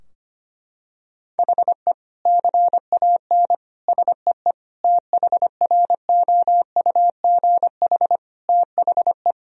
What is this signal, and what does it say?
Morse code, heard as a keyed tone, about 25 wpm, 700 Hz. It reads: HE CAN SEE THROUGH THE
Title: Sentences from Top 1000 Words - Review Part 2 - 25wpm